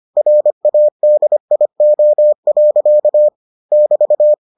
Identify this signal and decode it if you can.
Morse code, heard as a keyed tone, about 25 wpm, 600 Hz. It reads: RADIO. <BT>